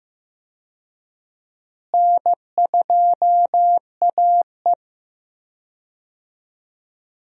Morse code, heard as a keyed tone, about 15 wpm, 700 Hz. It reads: N2AE
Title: USA Call Signs - 4 Characters - Repeated 3x - 15wpm